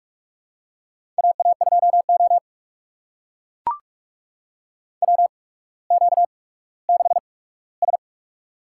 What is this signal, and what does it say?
Morse code, heard as a keyed tone, about 43 wpm, spaced as for 19 wpm, 700 Hz. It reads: AA2O WQ6S
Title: USA Call Signs - 4 Characters - 8x Character Spacing - 45wpm